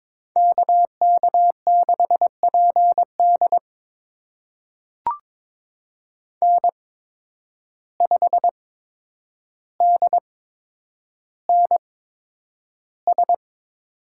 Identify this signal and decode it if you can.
Morse code, heard as a keyed tone, about 22 wpm, spaced as for 9 wpm, 700 Hz. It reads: KK6PD N5DNS
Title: USA Call Signs - 5 Characters - 8x Character Spacing - 22wpm